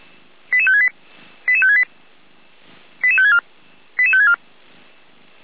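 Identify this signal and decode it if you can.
Five-tone selective call (ZVEI), heard as quick runs of five tones, tones of about 70 ms. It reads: 88568 88568 88564 88564
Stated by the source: ZVEI five-tone call 88190 received via SDR.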